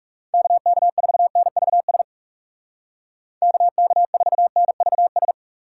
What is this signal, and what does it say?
Morse code, heard as a keyed tone, about 38 wpm, 700 Hz. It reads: KK4NVS KK4NVS